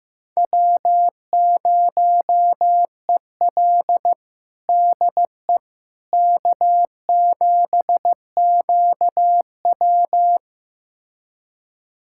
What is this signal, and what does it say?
Morse code, heard as a keyed tone, about 15 wpm, 700 Hz. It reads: W0EL DE K7QW
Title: USA Callsign DE Callsign - 4 Characters - 15wpm